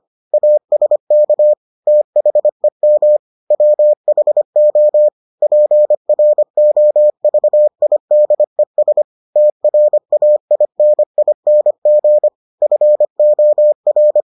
Morse code, heard as a keyed tone, about 25 wpm, 600 Hz. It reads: ASK THEM WHO PROVIDES TRAINING FOR